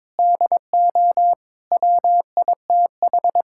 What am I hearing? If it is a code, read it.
Morse code, heard as a keyed tone, about 22 wpm, 700 Hz. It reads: DO WITH
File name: practice / 22wpm / Sets of 8 Words - 22wpm